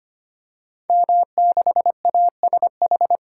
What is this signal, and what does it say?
Morse code, heard as a keyed tone, about 25 wpm, 700 Hz. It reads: M6ASH
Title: Code Groups - 5 Letter:Numbers - 25wpm